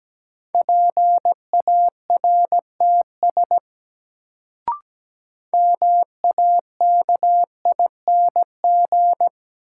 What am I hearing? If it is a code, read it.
Morse code, heard as a keyed tone, about 17 wpm, 700 Hz. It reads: PARTS MAKING
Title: Sentences from Top 1000 Words - Review Part 6 - New Word Drill - 17wpm